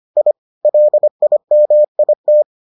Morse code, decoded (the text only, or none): I LIMIT